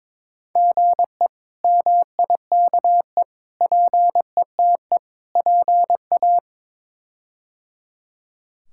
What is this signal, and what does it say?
Morse code, heard as a keyed tone, about 22 wpm, 700 Hz. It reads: GE MIKE PETE PA
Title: K1USN SST QSO Practice - 22wpm